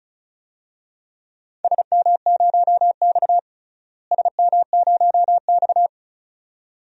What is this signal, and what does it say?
Morse code, heard as a keyed tone, about 34 wpm, 700 Hz. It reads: SM0X SM0X
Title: DXCC-100 Call Signs - 4 Characters - Repeated 2x - 35wpm